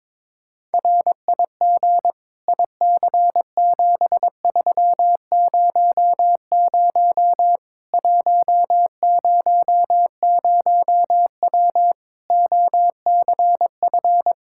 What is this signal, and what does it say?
Morse code, heard as a keyed tone, about 22 wpm, 700 Hz. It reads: RIG IC7300 100W OCF
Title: Practice QSO - Exchange 2 - Info - 22wpm